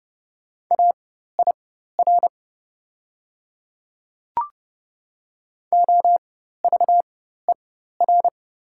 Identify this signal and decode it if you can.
Morse code, heard as a keyed tone, about 30 wpm, spaced as for 14 wpm, 690 Hz. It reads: AIR OVER
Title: 4x Character Spacing - Top 300 Words - 30wpm